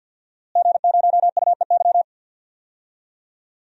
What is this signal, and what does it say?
Morse code, heard as a keyed tone, about 46 wpm, 700 Hz. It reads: G0UEY
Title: DXCC-100 Call Signs - 5 Characters - Repeated 2x - 50wpm